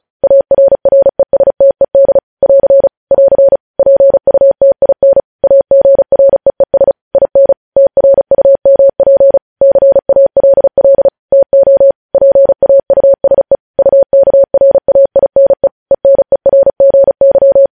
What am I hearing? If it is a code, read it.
Morse code, heard as a keyed tone, about 36 wpm, 550 Hz. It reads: ARRESTED <AR> <AR> PUTIN AGREES IN TRUMP CALL TO PAUSE UKRAINE ENERGY
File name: podcast / News headlines-1x-35x35